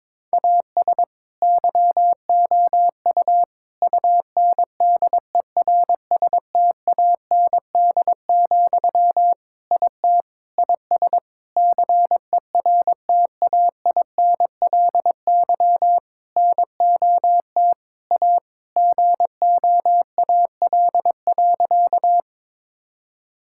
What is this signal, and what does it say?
Morse code, heard as a keyed tone, about 22 wpm, 700 Hz. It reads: AS YOU UNDERSTAND, IT IS CERTAINLY NOT A GOAL.
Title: Sentences from Top 1000 Words - Review Part 5 - 22wpm